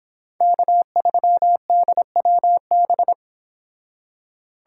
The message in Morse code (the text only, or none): K3DWB